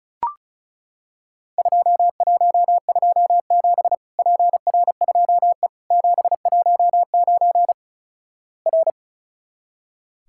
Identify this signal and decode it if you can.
Morse code, heard as a keyed tone, about 36 wpm, 700 Hz. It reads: 2127 PR2E 719 R
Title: Worked All Europe DX Contest - QTC Exchange - 35wpm